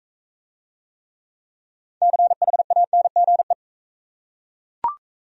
Morse code, heard as a keyed tone, about 40 wpm, 700 Hz. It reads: CHANGE